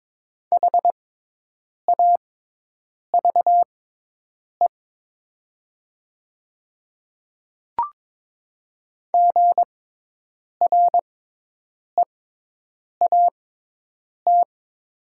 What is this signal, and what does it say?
Morse code, heard as a keyed tone, about 22 wpm, spaced as for 8 wpm, 700 Hz. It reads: HAVE GREAT
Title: 6x Character Spacing - Top 200 Words - 22wpm